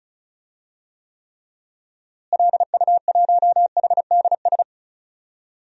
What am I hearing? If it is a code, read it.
Morse code, heard as a keyed tone, about 34 wpm, 700 Hz. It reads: LU1HDS
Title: DXCC-100 Call Signs - 6 Characters - Repeated 3x - 35wpm